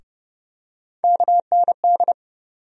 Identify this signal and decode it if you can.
Morse code, heard as a keyed tone, about 31 wpm, 700 Hz. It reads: KND
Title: USA Call Sign Postfixes - 3 Letters 30wpm